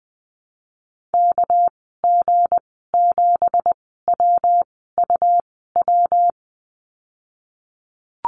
Morse code, heard as a keyed tone, about 20 wpm, spaced as for 15 wpm, 700 Hz. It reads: KG7WUW
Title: 207 Lesson 33g - Letter j - Call Signs 20wpm